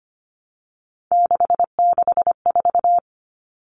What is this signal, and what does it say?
Morse code, heard as a keyed tone, about 25 wpm, 700 Hz. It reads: 664
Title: Common Cut Numbers - 3-Digit 25wpm